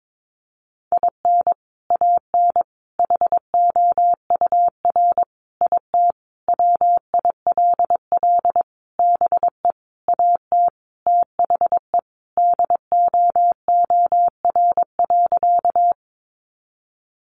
Morse code, heard as a keyed tone, about 22 wpm, 700 Hz. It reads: IN AN HOUR IT WILL BE AT THE DOOR.
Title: Sentences from Top 400 Words 22wpm